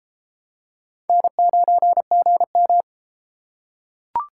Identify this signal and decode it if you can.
Morse code, heard as a keyed tone, about 32 wpm, 700 Hz. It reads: N9GM